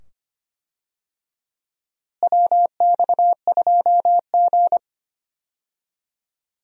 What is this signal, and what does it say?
Morse code, heard as a keyed tone, about 25 wpm, 700 Hz. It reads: WX2G
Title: USA Call Signs - 4 Characters - Repeated 2x - 25wpm